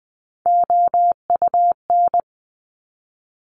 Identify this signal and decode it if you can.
Morse code, heard as a keyed tone, about 20 wpm, 700 Hz. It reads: OUN
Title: Top Three Letter Combinations in Words 20wpm